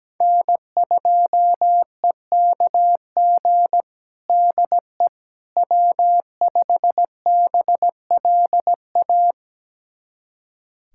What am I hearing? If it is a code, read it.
Morse code, heard as a keyed tone, about 17 wpm, 700 Hz. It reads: N2EKG DE W5BLA
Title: USA Callsign DE Callsign - 5 Characters - 17wpm